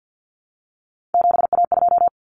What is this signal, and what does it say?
Morse code, heard as a keyed tone, about 50 wpm, 700 Hz. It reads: 7A2
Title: Common Cut Numbers - 3-Digit 50wpm